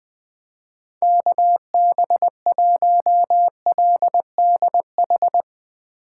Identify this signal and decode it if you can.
Morse code, heard as a keyed tone, about 20 wpm, 700 Hz. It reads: KB1LDH
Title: Speed Racing - USA 2x3 Call Signs 17wpm